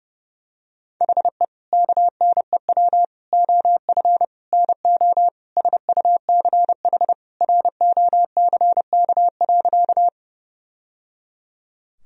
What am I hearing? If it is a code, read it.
Morse code, heard as a keyed tone, about 30 wpm, 700 Hz. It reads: HE KNEW OF NO SUCH ROCK.
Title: Sentences from Top 1000 Words - Review Part 2 - 30wpm